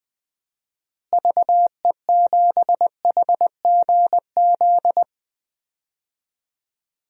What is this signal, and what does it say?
Morse code, heard as a keyed tone, about 20 wpm, 700 Hz. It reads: VE7HGZ
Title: DXCC-100 Call Signs - 6 Characters - Repeated 2x - 20wpm